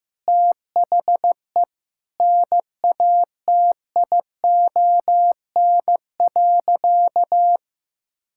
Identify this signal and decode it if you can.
Morse code, heard as a keyed tone, about 15 wpm, 700 Hz. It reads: THE NATION.
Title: Sentences from Top 1000 Words - Review Part 6 - 15wpm